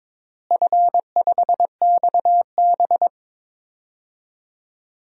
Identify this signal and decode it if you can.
Morse code, heard as a keyed tone, about 22 wpm, 700 Hz. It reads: F5XB
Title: DXCC-100 Call Signs - 4 Characters - Repeated 3x - 22wpm